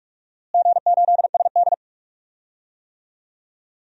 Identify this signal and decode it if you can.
Morse code, heard as a keyed tone, about 41 wpm, 700 Hz. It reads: G8SD
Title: DXCC-100 Call Signs - 4 Characters - Repeated 2x - 45wpm